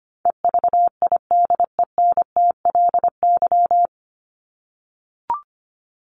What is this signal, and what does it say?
Morse code, heard as a keyed tone, about 25 wpm, 700 Hz. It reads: EVIDENTLY
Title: LY Suffix - Top 100 Words with Base Words 25wpm